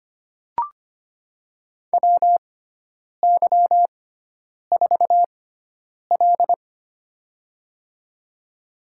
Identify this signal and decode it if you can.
Morse code, heard as a keyed tone, about 25 wpm, spaced as for 9 wpm, 700 Hz. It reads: WY4L
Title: USA Call Signs - 6x Character Spacing - 25wpm